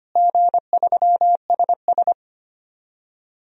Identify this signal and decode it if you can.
Morse code, heard as a keyed tone, about 25 wpm, 700 Hz. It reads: G3SS
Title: DXCC-100 Call Signs - 4 Characters - Repeated 3x - 25wpm